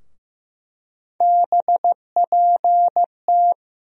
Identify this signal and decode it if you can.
Morse code, heard as a keyed tone, about 15 wpm, 700 Hz. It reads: BPT
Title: USA Call Sign Postfixes - 3 Letters 15wpm